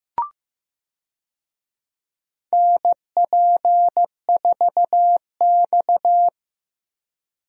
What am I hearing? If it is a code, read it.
Morse code, heard as a keyed tone, about 15 wpm, 700 Hz. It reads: NP4X
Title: USA Call Signs - 4 Characters - Repeated 3x - 15wpm